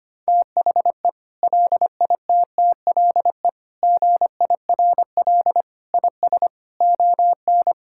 Morse code, heard as a keyed tone, about 25 wpm, 700 Hz. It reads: THE LITTLE GIRL IS ON
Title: Sentences from Top 1600 Words 25wpm